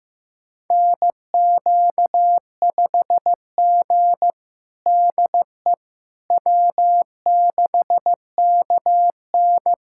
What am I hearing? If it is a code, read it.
Morse code, heard as a keyed tone, about 16 wpm, 700 Hz. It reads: NQ5G DE W6KN
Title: USA Callsign DE Callsign - 4 Characters - 15wpm